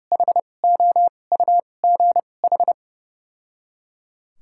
Morse code, heard as a keyed tone, about 30 wpm, spaced as for 22 wpm, 700 Hz. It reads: HOUGH
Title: 2x Character Spacing - Top 500 Words - 30wpm